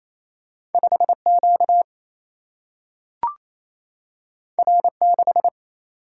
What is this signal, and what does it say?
Morse code, heard as a keyed tone, about 28 wpm, 700 Hz. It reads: 5 Q R 6
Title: Single-Letter Single-Number 28wpm